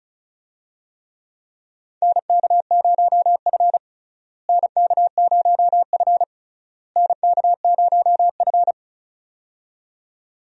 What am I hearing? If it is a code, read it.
Morse code, heard as a keyed tone, about 36 wpm, 700 Hz. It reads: NK0F NK0F NK0F
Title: USA Call Signs - 4 Characters - Repeated 3x - 35wpm